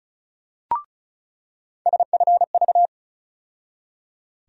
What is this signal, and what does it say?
Morse code, heard as a keyed tone, about 36 wpm, 700 Hz. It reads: SFV
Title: USA Call Sign Postfixes - 3 Letters 35wpm